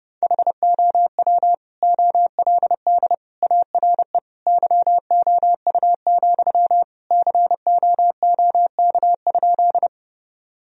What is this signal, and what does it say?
Morse code, heard as a keyed tone, about 30 wpm, 700 Hz. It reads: HOW OLD ARE YOU, COOK?
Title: Sentences from Top 1000 Words - Review Part 4 - 30wpm